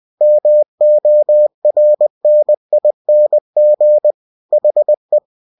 Morse code, heard as a keyed tone, about 20 wpm, 600 Hz. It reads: MORNING HE
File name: cw-books / winnie-the-pooh / winnie-20wpm-page-0096